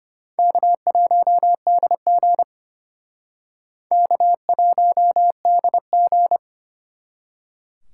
Morse code, heard as a keyed tone, about 31 wpm, 700 Hz. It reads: K1DG K1DG
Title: Speed Racing - USA 1x2 Call Signs 25wpm